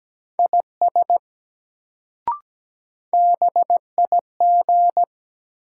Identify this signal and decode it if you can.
Morse code, heard as a keyed tone, about 17 wpm, 700 Hz. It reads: IS BIG
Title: Sets of 2 Words - New Word Drill - 17wpm